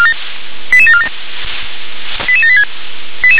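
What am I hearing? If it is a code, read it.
Five-tone selective call (ZVEI), heard as quick runs of five tones, tones of about 70 ms. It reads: 88747 88786
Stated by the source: ZVEI five-tone selective call received via SDR.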